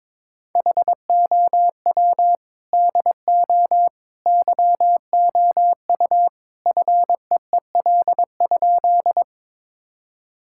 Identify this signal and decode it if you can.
Morse code, heard as a keyed tone, about 22 wpm, 700 Hz. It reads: HOW DO YOU FEEL?
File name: practice / 22wpm / Sentences with 3-Word Phrases - 22wpm